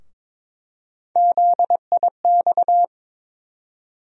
Morse code, heard as a keyed tone, about 22 wpm, 700 Hz. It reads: ZIX
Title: USA Call Sign Postfixes - 3 Letters 22wpm